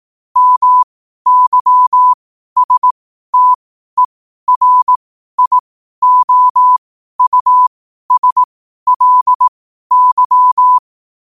Morse code, heard as a keyed tone, about 18 wpm, spaced as for 12 wpm, 1000 Hz. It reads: MYSTERIOUSLY